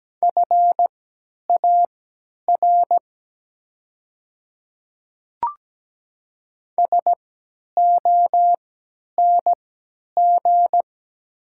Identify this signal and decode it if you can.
Morse code, heard as a keyed tone, about 17 wpm, spaced as for 10 wpm, 700 Hz. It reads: FAR SONG
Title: 3x Character Spacing - Top 500 Words - 17wpm